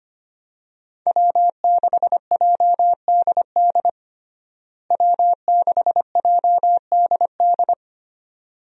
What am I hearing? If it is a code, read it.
Morse code, heard as a keyed tone, about 25 wpm, 700 Hz. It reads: W6JDD W6JDD
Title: USA Call Signs - 5 Characters - Repeated 2x - 25wpm